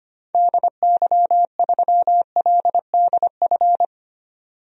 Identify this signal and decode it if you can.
Morse code, heard as a keyed tone, about 25 wpm, 700 Hz. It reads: DY3LDF
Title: DXCC-100 Call Signs - 6 Characters - 25wpm